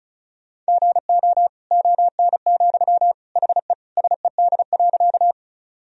Morse code, heard as a keyed tone, about 34 wpm, 700 Hz. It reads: GO ON, HE SED.
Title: Sentences from Top 100 Words and Abbreviations 35wpm